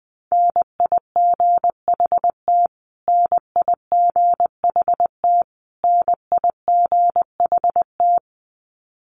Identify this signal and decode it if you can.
Morse code, heard as a keyed tone, about 20 wpm, 700 Hz. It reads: NIGHT NIGHT NIGHT
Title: Top 500 Words Repeated 20wpm